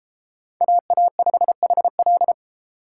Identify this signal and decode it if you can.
Morse code, heard as a keyed tone, about 32 wpm, 700 Hz. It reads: AA5HL